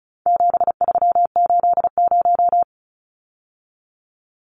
Morse code, heard as a keyed tone, about 34 wpm, 700 Hz. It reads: 7380
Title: Numbers - 4-Digit 35wpm